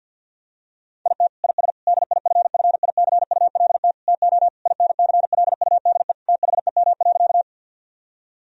Morse code, heard as a keyed tone, about 50 wpm, 700 Hz. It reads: IT IS DIFFICULT TO INCLUDE THEM.
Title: Sentences from Top 1000 Words - Review Part 3 - 50wpm